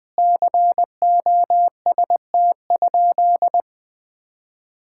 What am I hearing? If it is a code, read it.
Morse code, heard as a keyed tone, about 21 wpm, 700 Hz. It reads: COST?
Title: Sentences from Top 1000 Words - Review Part 6 - 20wpm